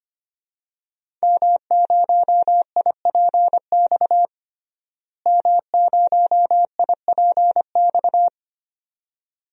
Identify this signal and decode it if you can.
Morse code, heard as a keyed tone, about 25 wpm, 700 Hz. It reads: M0IPX M0IPX
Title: DXCC-100 Call Signs - 5 Characters - Repeated 2x - 25wpm